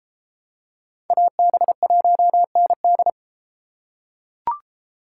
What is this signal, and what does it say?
Morse code, heard as a keyed tone, about 30 wpm, 700 Hz. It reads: AB1ND